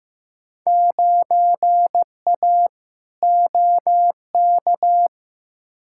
Morse code, heard as a keyed tone, about 15 wpm, 700 Hz. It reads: 9A OK
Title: Field Day Exchanges Encore - 15wpm